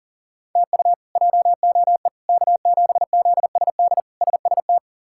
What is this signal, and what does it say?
Morse code, heard as a keyed tone, about 40 wpm, 700 Hz. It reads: TU JOE K7ZSD SST
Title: K1USN SST QSO Practice - 40wpm